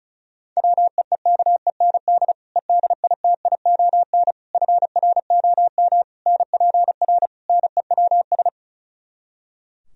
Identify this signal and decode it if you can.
Morse code, heard as a keyed tone, about 32 wpm, 700 Hz. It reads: WEEKEND EDITION FROM NPR NEWS
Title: Speed-Racing - Sets of 5 Words 35wpm